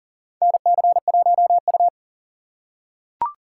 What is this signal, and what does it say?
Morse code, heard as a keyed tone, about 40 wpm, 700 Hz. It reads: NC1U
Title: USA Call Signs - 6x Character Spacing - 40wpm